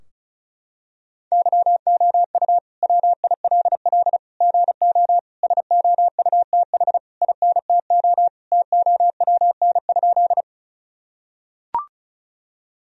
Morse code, heard as a keyed tone, about 36 wpm, 700 Hz. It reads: YOU WILL GO SOUTH INTO TOWN?
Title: Sentences from Top 1000 Words - Review Part 2 - 35wpm